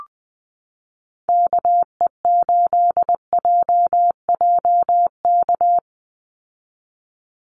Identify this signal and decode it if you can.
Morse code, heard as a keyed tone, about 20 wpm, 700 Hz. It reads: KE8JJK
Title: USA Call Signs 20wpm